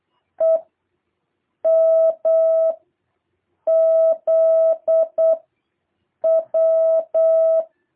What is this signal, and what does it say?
Morse code, heard as a keyed tone, about 8 wpm, 660 Hz. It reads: E M Z W
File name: jh8cbh / 2023cw